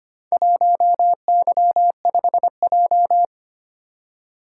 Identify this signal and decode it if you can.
Morse code, heard as a keyed tone, about 25 wpm, 700 Hz. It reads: 1Y5J
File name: practice / 25wpm / Four Letter-Number Combinations - 25wpm